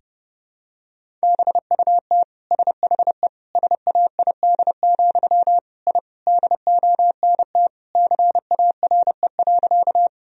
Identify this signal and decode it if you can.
Morse code, heard as a keyed tone, about 31 wpm, 700 Hz. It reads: BUT SHE SAID, I DONT CARE.
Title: Sentences from Top 1000 Words - Review Part 1 - 30wpm